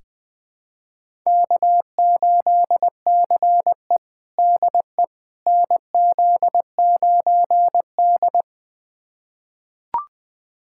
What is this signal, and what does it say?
Morse code, heard as a keyed tone, about 21 wpm, 700 Hz. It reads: K8CE DE NZ9D
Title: USA Callsign DE Callsign - 4 Characters - 20wpm